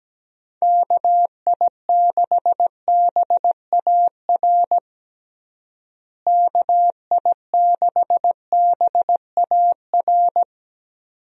Morse code, heard as a keyed tone, about 17 wpm, 700 Hz. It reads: KI6BAR KI6BAR
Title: USA Call Signs - 6 Characters - Repeated 2x - 17wpm